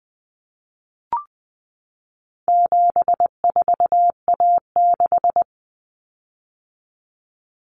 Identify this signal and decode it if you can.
Morse code, heard as a keyed tone, about 20 wpm, 700 Hz. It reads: 74A6
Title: Common Cut Numbers - 4-Digit 20wpm